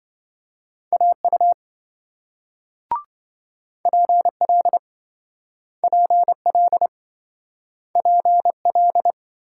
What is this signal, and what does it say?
Morse code, heard as a keyed tone, about 27 wpm, 700 Hz. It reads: AU PL PL PL